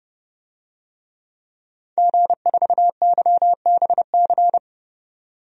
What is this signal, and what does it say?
Morse code, heard as a keyed tone, about 30 wpm, 700 Hz. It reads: G4YBC
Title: DXCC-100 Call Signs - 5 Characters - 30wpm